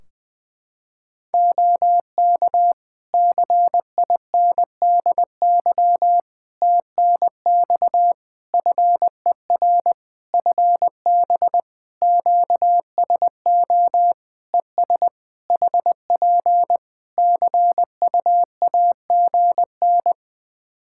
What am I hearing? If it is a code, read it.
Morse code, heard as a keyed tone, about 20 wpm, 700 Hz. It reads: OK CINDY TNX FER FB QSO ES HP CUAGN
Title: K1USN SST QSO Practice 20wpm